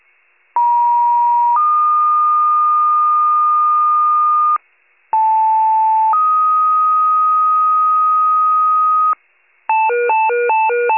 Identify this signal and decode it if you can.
Two-tone page, heard as a one-second tone, then a three-second tone, about 950 then 1220 Hz, about 880 then 1250 Hz. It is heard from 0.6 to 9.1 s.
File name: wcalert